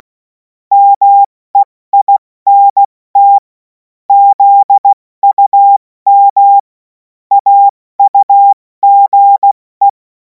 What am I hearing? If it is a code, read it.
Morse code, heard as a keyed tone, about 16 wpm, 800 Hz. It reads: MEINT ZUM AUGE
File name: Morsekurs / Witze / Witz 17. Februar 2026 70 F80